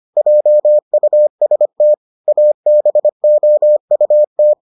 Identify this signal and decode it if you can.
Morse code, heard as a keyed tone, about 25 wpm, 600 Hz. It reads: JUST ABOUT